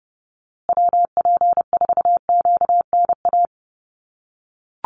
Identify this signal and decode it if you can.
Morse code, heard as a keyed tone, about 30 wpm, 700 Hz. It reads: WP4QNA
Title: USA Call Signs Repeated 3x 30wpm